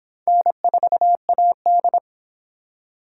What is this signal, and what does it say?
Morse code, heard as a keyed tone, about 24 wpm, 700 Hz. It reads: N4AD